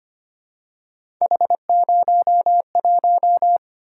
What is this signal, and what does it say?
Morse code, heard as a keyed tone, about 25 wpm, 700 Hz. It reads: H01
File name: Three Letter-Number Combinations - 25wpm